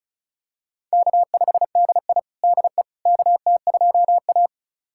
Morse code, heard as a keyed tone, about 36 wpm, 700 Hz. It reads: K5DI DE KT2A
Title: USA Callsign DE Callsign - 4 Characters - 35wpm